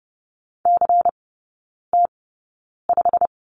Morse code, heard as a keyed tone, about 30 wpm, spaced as for 9 wpm, 700 Hz. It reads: CT5
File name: Lesson 16e - Letter c - Three Characters 30wpm